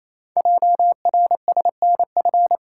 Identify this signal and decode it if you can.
Morse code, heard as a keyed tone, about 29 wpm, 700 Hz. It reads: JRSNF
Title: Code Groups - 5 Letters - 28wpm